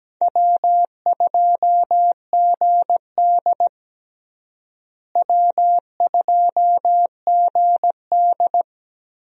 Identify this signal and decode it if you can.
Morse code, heard as a keyed tone, about 17 wpm, 700 Hz. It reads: W2GD W2GD
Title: USA Call Signs - 4 Characters - Repeated 2x - 17wpm